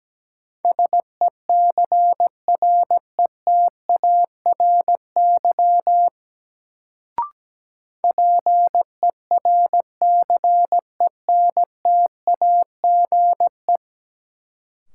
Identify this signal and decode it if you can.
Morse code, heard as a keyed tone, about 17 wpm, 700 Hz. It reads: SECRETARY PERCENTAGE
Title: Sentences from Top 1300 Words - New Word Drill - 17wpm